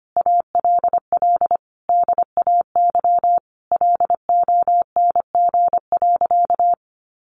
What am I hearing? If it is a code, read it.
Morse code, heard as a keyed tone, about 25 wpm, 700 Hz. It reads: ALL DAY LONG.
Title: Sentences from Top 1400 Words 25wpm